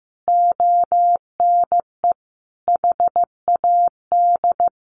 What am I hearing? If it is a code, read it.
Morse code, heard as a keyed tone, about 15 wpm, 700 Hz. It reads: ONE HAD
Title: Sentences from Top 1200 Words 15wpm